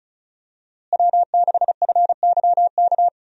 Morse code, heard as a keyed tone, about 34 wpm, 700 Hz. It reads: W6FYK
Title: USA Call Signs - 5 Characters - Repeated 2x - 35wpm